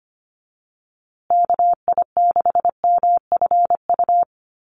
Morse code, heard as a keyed tone, about 25 wpm, 700 Hz. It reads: KI6MFU
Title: USA Call Signs Repeated 3x 25wpm